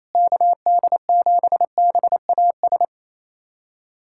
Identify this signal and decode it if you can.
Morse code, heard as a keyed tone, about 27 wpm, 700 Hz. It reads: KD7BAS
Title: USA Callsign DE Callsign - 6 Characters - 28wpm